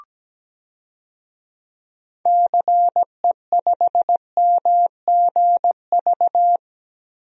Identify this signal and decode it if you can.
Morse code, heard as a keyed tone, about 17 wpm, 700 Hz. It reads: CE5MGV
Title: DXCC-100 Call Signs - 6 Characters - Repeated 3x - 17wpm